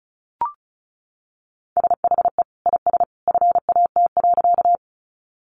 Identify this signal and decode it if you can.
Morse code, heard as a keyed tone, about 34 wpm, 700 Hz. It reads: SHE IS FAT.
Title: Sentences from Top 600 Words 35wpm